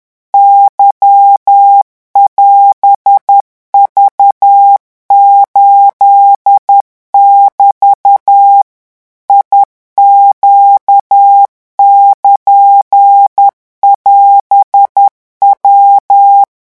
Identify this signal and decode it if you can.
Morse code, heard as a keyed tone, about 10 wpm, 800 Hz. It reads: Y<AS>V8<BT> IQ<KN><AS>W
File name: morse mp3 / revision16-vitesse 10 mots